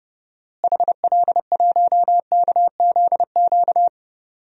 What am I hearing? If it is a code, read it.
Morse code, heard as a keyed tone, about 32 wpm, 700 Hz. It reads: HL1KZQ
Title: DXCC-100 Call Signs - 6 Characters - Repeated 2x - 30wpm